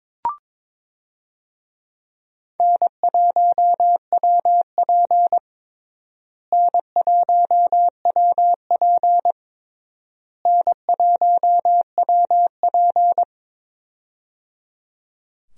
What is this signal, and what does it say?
Morse code, heard as a keyed tone, about 22 wpm, 700 Hz. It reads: N1WP N1WP N1WP
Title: USA Call Signs - 4 Characters - Repeated 3x - 22wpm